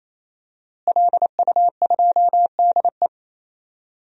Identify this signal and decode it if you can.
Morse code, heard as a keyed tone, about 28 wpm, 700 Hz. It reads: LU2DE
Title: DXCC-100 Call Signs - 5 Characters - Repeated 3x - 28wpm